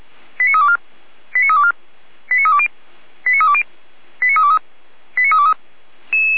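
Five-tone selective call (ZVEI), heard as quick runs of five tones, tones of about 70 ms. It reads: 89324 89324 89320 89320 89323 89323
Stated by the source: ZVEI five-tone selective call received on an SDR, logged by the Feuerwehr Orlamünde fire brigade server.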